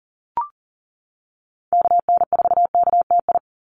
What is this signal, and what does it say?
Morse code, heard as a keyed tone, about 40 wpm, 700 Hz. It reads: KN4KTI
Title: USA Call Signs 40wpm